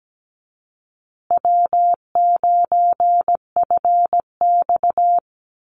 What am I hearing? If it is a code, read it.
Morse code, heard as a keyed tone, about 17 wpm, 700 Hz. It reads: W9FX
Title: Easy Call Signs Repeated 3x - 17wpm